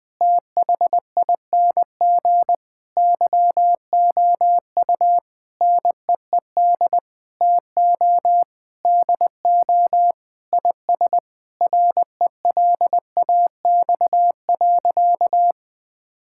Morse code, heard as a keyed tone, about 21 wpm, 700 Hz. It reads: THING YOU NEED TO DO IS RELAX.
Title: Sentences with 8-Word Phrases - 20wpm